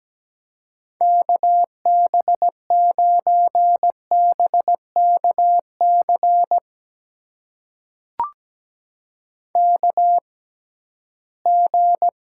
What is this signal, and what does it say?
Morse code, heard as a keyed tone, about 17 wpm, spaced as for 13 wpm, 700 Hz. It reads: KB9BKC KG
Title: USA Call Signs - 6 Characters - 6x Character Spacing - 17wpm